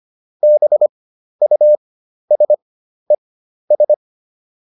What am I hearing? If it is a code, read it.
Morse code, heard as a keyed tone, about 25 wpm, spaced as for 12 wpm, 600 Hz. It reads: BUSES